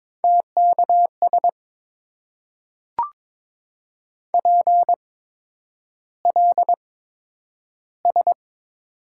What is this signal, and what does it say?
Morse code, heard as a keyed tone, about 23 wpm, spaced as for 10 wpm, 700 Hz. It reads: TKS PLS
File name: Top 90 QSO Elements - 8x Character Spacing - 22wpm